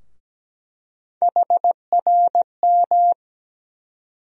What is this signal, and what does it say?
Morse code, heard as a keyed tone, about 17 wpm, 700 Hz. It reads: HRM
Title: USA Call Sign Postfixes - 3 Letters 17wpm